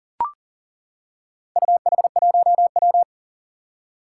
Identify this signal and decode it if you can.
Morse code, heard as a keyed tone, about 37 wpm, 700 Hz. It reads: UH1W
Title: Four Challenging Letter-Number Combinations 40wpm